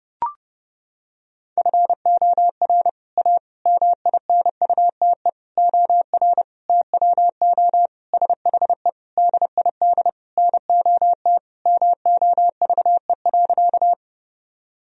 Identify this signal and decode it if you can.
Morse code, heard as a keyed tone, about 30 wpm, 700 Hz. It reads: FOR A MINUTE OR TWO SHE DID NOT MOVE.
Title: Sentences from Top 1000 Words - Review Part 1 - 30wpm